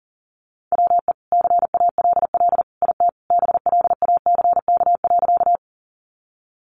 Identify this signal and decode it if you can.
Morse code, heard as a keyed tone, about 40 wpm, 700 Hz. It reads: WE CALL IT BLACK.
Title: Sentences from Top 400 Words 40wpm